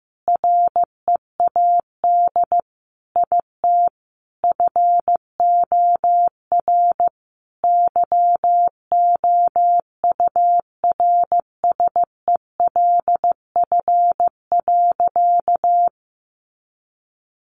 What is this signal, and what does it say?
Morse code, heard as a keyed tone, about 15 wpm, 700 Hz. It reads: READ IT FOR YOURSELF.
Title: Sentences from Top 600 Words 15wpm